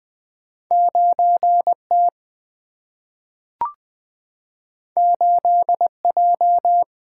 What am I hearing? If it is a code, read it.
Morse code, heard as a keyed tone, about 20 wpm, 700 Hz. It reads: 9T 8J
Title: Single-Letter Single-Number 20wpm